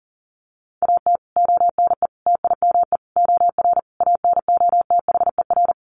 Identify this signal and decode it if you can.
Morse code, heard as a keyed tone, about 40 wpm, 700 Hz. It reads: AT ONE TIME OR ANOTHER
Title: Sets of 5 Words Encore 40wpm